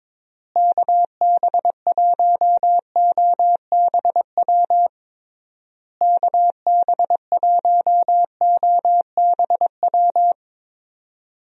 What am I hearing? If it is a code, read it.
Morse code, heard as a keyed tone, about 22 wpm, 700 Hz. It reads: KB1OBW KB1OBW
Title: USA Call Signs - 6 Characters - Repeated 3x - 22wpm